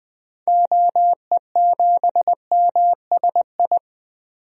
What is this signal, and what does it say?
Morse code, heard as a keyed tone, about 20 wpm, 700 Hz. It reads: OE7MSI
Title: DXCC-100 Call Signs - 6 Characters - 20wpm